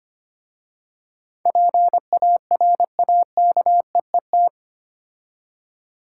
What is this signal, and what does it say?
Morse code, heard as a keyed tone, about 25 wpm, 700 Hz. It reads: PARAKEET
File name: practice / 25wpm / Difficult Words with Hints - 25wpm